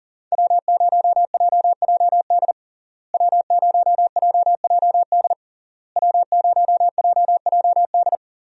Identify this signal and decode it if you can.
Morse code, heard as a keyed tone, about 37 wpm, 700 Hz. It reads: W0JJD W0JJD W0JJD
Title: USA Call Signs - 5 Characters - Repeated 3x - 40wpm